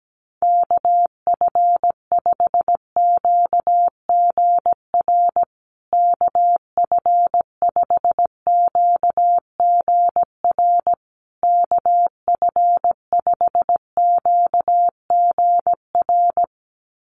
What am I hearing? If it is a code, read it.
Morse code, heard as a keyed tone, about 17 wpm, 700 Hz. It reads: KF5QGR KF5QGR KF5QGR
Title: USA Call Signs Repeated 3x 17wpm